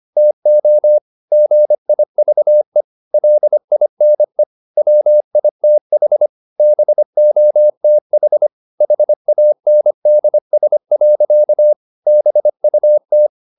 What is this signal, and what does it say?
Morse code, heard as a keyed tone, about 25 wpm, 600 Hz. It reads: TO GIVE LINE WITH BOTH HANDS. BUT